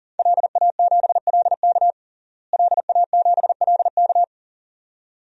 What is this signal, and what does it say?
Morse code, heard as a keyed tone, about 40 wpm, 700 Hz. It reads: LA7LK LA7LK
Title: DXCC-100 Call Signs - 5 Characters - Repeated 2x - 40wpm